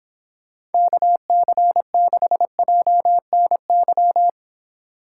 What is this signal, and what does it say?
Morse code, heard as a keyed tone, about 29 wpm, 700 Hz. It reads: KC6JNY